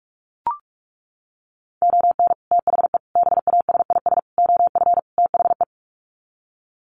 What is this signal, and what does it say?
Morse code, heard as a keyed tone, about 44 wpm, 700 Hz. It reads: ON THE BASIS OF THE
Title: Sets of 5 Words 45wpm